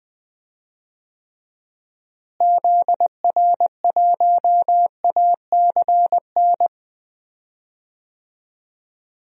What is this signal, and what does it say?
Morse code, heard as a keyed tone, about 20 wpm, 700 Hz. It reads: ZR1ACN
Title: DXCC-100 Call Signs - 6 Characters - Repeated 3x - 20wpm